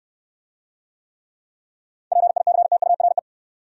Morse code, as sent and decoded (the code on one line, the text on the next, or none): .--. .. -.-. - ..- .-. .
PICTURE